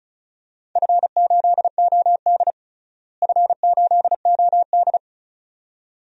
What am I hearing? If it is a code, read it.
Morse code, heard as a keyed tone, about 36 wpm, 690 Hz. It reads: F8OD F8OD
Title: DXCC-100 Call Signs - 4 Characters - Repeated 2x - 35wpm